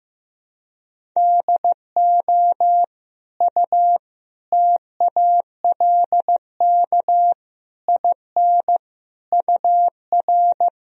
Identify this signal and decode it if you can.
Morse code, heard as a keyed tone, about 15 wpm, 700 Hz. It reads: DO U TALK IN UR
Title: Sentences from Top 500 Words and Abbreviations 15wpm